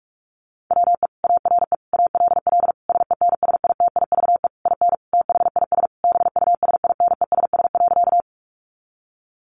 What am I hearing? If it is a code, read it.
Morse code, heard as a keyed tone, about 44 wpm, 700 Hz. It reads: WE ARE ALL SENSITIVE IN THIS BUSINESS.
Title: Sentences from Top 1400 Words 45wpm